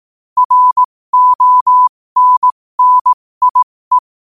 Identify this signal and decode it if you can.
Morse code, heard as a keyed tone, about 18 wpm, spaced as for 15 wpm, 1000 Hz. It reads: RONNIE
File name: qotdfiles / 15wpm